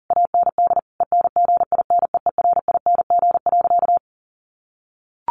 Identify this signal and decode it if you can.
Morse code, heard as a keyed tone, about 40 wpm, 700 Hz. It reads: AND ENGINEERING.
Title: Sentences from Top 1500 Words 40wpm